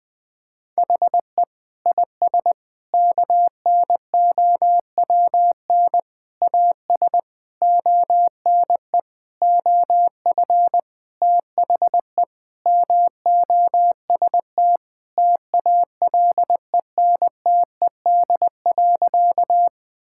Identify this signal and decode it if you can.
Morse code, heard as a keyed tone, about 20 wpm, 700 Hz. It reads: HE IS KNOWN AS ONE OF THE MOST TALENTED.
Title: Sentences with 5-Word Phrases - 20wpm